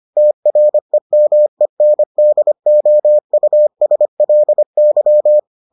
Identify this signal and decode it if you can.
Morse code, heard as a keyed tone, about 25 wpm, 600 Hz. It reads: TREMENDOUSLY